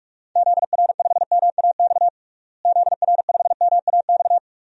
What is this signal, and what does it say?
Morse code, heard as a keyed tone, about 46 wpm, 700 Hz. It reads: ZR5MAX ZR5MAX
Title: DXCC-100 Call Signs - 6 Characters - Repeated 3x - 45wpm